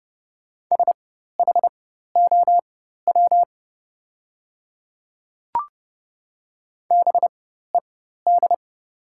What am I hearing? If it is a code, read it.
Morse code, heard as a keyed tone, about 30 wpm, spaced as for 14 wpm, 690 Hz. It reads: SHOW BED
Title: 4x Character Spacing - Top 500 Words - 30wpm